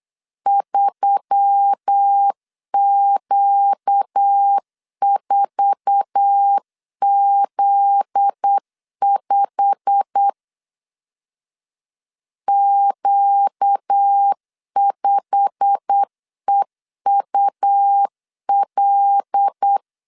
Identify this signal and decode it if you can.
Morse code, heard as a keyed tone, about 8 wpm, 800 Hz. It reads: 3Q4Z5 Q5EUL
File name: LEZIONI / 8 YJ45 / 8 YJ45 8wpm 3214